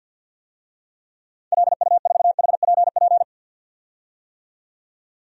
Morse code, heard as a keyed tone, about 53 wpm, 700 Hz. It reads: LU4HPP
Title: DXCC-100 Call Signs - 6 Characters - 50wpm